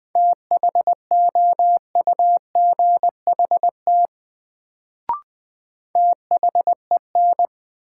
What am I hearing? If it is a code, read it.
Morse code, heard as a keyed tone, about 20 wpm, 700 Hz. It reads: THOUGHT THEN
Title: Sets of 2 Words - New Word Drill - 20wpm